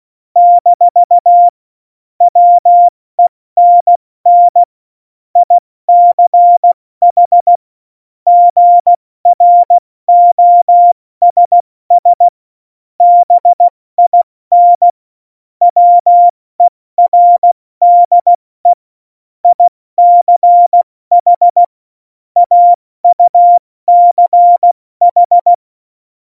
Morse code, heard as a keyed tone, about 16 wpm, spaced as for 14 wpm, 700 Hz. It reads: - WENN ICH GROSS BIN WERDE ICH AUCH
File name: Morsekurs / Witze / Witz 26. Juni 2018, 70 F80